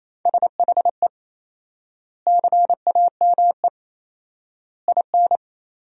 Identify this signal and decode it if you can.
Morse code, heard as a keyed tone, about 28 wpm, 700 Hz. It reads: SHE CAME IN
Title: 4x Word-Spacing - Sentences from Top 300 Words - 28wpm